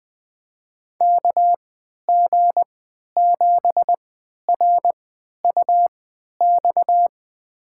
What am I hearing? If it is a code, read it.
Morse code, heard as a keyed tone, about 20 wpm, spaced as for 11 wpm, 700 Hz. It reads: KG7RUX
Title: USA Call Signs - 6 Characters - 3x Character Spacing - 20wpm